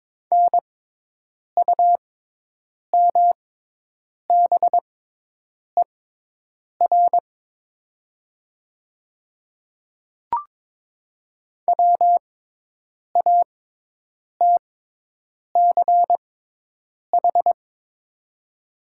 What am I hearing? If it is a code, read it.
Morse code, heard as a keyed tone, about 22 wpm, spaced as for 8 wpm, 700 Hz. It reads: NUMBER WATCH
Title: 6x Character Spacing - Top 300 Words - 22wpm